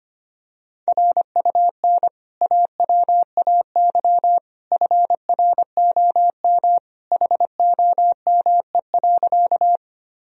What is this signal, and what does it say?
Morse code, heard as a keyed tone, about 25 wpm, 700 Hz. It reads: RUN AWAY FROM HOME.
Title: Sentences from Top 1000 Words - Review Part 1 - 25wpm